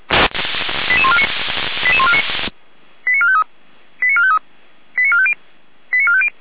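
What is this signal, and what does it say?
Five-tone selective call (ZVEI), heard as quick runs of five tones, tones of about 70 ms. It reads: 88149 88149 89453 89453 89450 89450
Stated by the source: ZVEI five-tone call received by SDR, from the Feuerwehr Orlamünde audio server.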